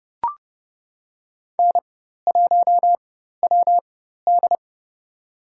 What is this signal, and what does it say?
Morse code, heard as a keyed tone, about 30 wpm, spaced as for 14 wpm, 700 Hz. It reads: N1WD
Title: USA Call Signs - 4x Character Spacing - 30wpm